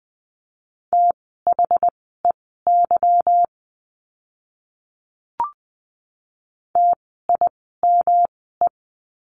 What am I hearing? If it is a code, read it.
Morse code, heard as a keyed tone, about 20 wpm, spaced as for 15 wpm, 700 Hz. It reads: THEY TIME
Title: 146 Lesson 24f - Letter y - Words 20wpm